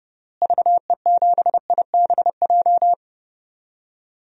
Morse code, heard as a keyed tone, about 32 wpm, 700 Hz. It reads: VE7IBJ
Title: DXCC-100 Call Signs - 6 Characters - Repeated 3x - 30wpm